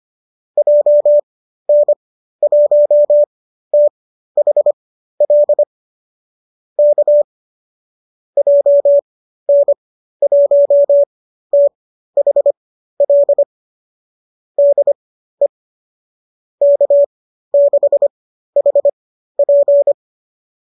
Morse code, heard as a keyed tone, about 25 wpm, spaced as for 13 wpm, 570 Hz. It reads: JN1THL K JN1THL DE K6HP